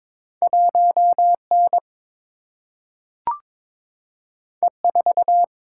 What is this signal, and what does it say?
Morse code, heard as a keyed tone, about 22 wpm, 700 Hz. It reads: 1 N E 4
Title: Single-Letter Single-Number 22wpm